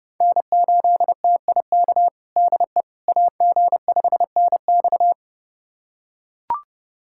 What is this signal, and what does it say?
Morse code, heard as a keyed tone, about 29 wpm, 700 Hz. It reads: N8TIK DE AG5NX
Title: USA Callsign DE Callsign - 5 Characters - 30wpm